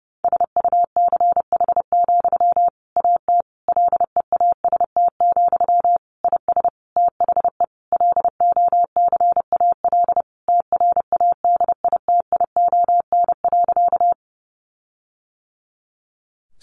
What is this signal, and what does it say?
Morse code, heard as a keyed tone, about 30 wpm, 700 Hz. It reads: SUCH, AT LEAST, IS THE LOCAL TRADITION.
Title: Sentences from Top 1500 Words 30wpm